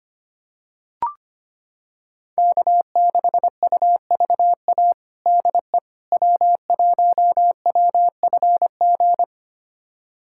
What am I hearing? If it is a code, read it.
Morse code, heard as a keyed tone, about 25 wpm, 700 Hz. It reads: K6UVA DE W1WFG
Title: USA Callsign DE Callsign - 5 Characters - 25wpm